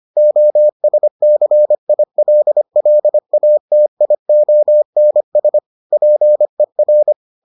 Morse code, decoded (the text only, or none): OSCILLATIONS PER